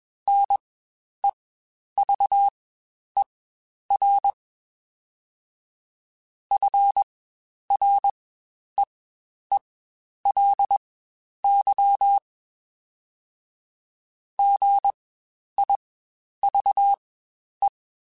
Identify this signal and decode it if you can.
Morse code, heard as a keyed tone, about 21 wpm, spaced as for 10 wpm, 800 Hz. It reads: NEVER FREELY GIVE
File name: Quotes Spaced